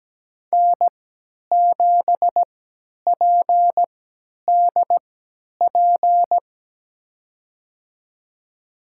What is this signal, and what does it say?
Morse code, heard as a keyed tone, about 17 wpm, spaced as for 10 wpm, 700 Hz. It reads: N7PDP
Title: USA Call Signs - 5 Characters - 3x Character Spacing - 17wpm